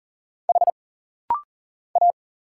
Morse code, heard as a keyed tone, about 40 wpm, 700 Hz. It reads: H A
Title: Single Letters - Rapid-Fire 40wpm